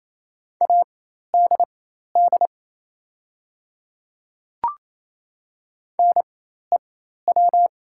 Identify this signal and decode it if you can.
Morse code, heard as a keyed tone, about 27 wpm, spaced as for 13 wpm, 690 Hz. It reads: ADD NEW
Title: 4x Character Spacing - Top 300 Words - 28wpm